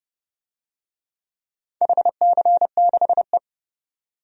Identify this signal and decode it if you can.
Morse code, heard as a keyed tone, about 30 wpm, 700 Hz. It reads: HC6E
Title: DXCC-100 Call Signs - 4 Characters - 30wpm